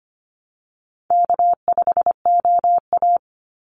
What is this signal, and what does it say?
Morse code, heard as a keyed tone, about 25 wpm, 700 Hz. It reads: K5OA
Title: Easy Call Signs - 25wpm